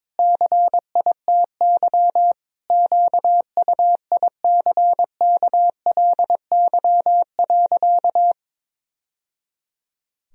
Morse code, heard as a keyed tone, about 22 wpm, 700 Hz. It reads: CITY QUICKLY.
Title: Sentences from Top 1000 Words - Review Part 1 - 22wpm